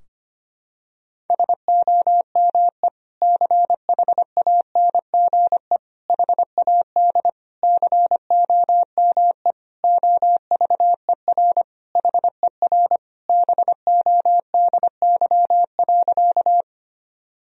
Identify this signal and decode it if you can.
Morse code, heard as a keyed tone, about 25 wpm, 700 Hz. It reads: SOME CHANGE HAD COME OVER HER BODY.
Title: Sentences from Top 1000 Words - Review Part 2 - 25wpm